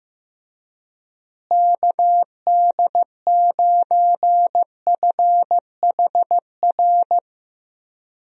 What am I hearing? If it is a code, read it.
Morse code, heard as a keyed tone, about 15 wpm, 700 Hz. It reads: KD9FHR
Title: USA Call Signs - 6 Characters - Repeated 2x - 15wpm